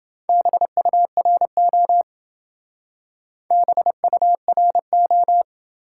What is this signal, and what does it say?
Morse code, heard as a keyed tone, about 28 wpm, 700 Hz. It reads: BURO BURO